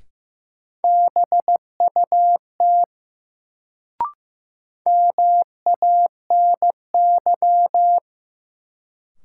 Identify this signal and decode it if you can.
Morse code, heard as a keyed tone, about 15 wpm, 700 Hz. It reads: BUT MANY
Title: Sentences from Top 100 Words - New Word Drill - 15wpm